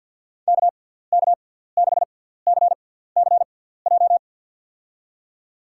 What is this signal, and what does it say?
Morse code, heard as a keyed tone, about 53 wpm, spaced as for 17 wpm, 700 Hz. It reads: KK6CCJ
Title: USA Call Signs - 6 Characters - 6x Character Spacing - 50wpm